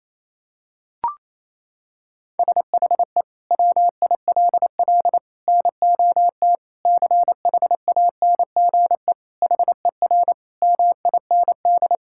Morse code, heard as a keyed tone, about 28 wpm, 700 Hz. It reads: SHE WILL NOT CHANGE HER MIND
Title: Sentences from Top 1000 Words - Review Part 6 - 28wpm